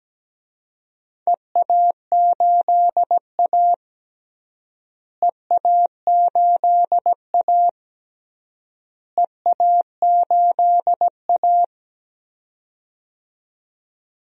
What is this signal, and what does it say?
Morse code, heard as a keyed tone, about 17 wpm, 700 Hz. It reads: EA8A EA8A EA8A
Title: DXCC-100 Call Signs - 4 Characters - Repeated 3x - 17wpm